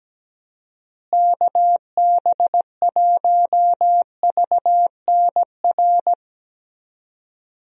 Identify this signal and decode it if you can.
Morse code, heard as a keyed tone, about 17 wpm, 700 Hz. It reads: KB1VNR
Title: USA Call Signs - 6 Characters - Repeated 3x - 17wpm